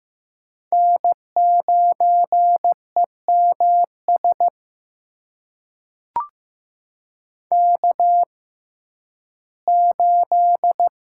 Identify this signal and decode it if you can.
Morse code, heard as a keyed tone, about 16 wpm, spaced as for 11 wpm, 700 Hz. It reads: N9EMS K8
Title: USA Call Signs - 5 Characters - 6x Character Spacing - 15wpm